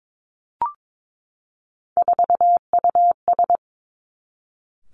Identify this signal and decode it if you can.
Morse code, heard as a keyed tone, about 22 wpm, 700 Hz. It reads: 4US
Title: Three Challenging Letter-Number Combinations 22wpm